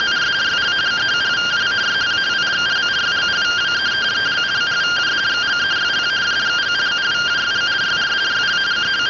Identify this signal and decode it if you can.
Two-tone teleprinter signal, about 170 Hz shift.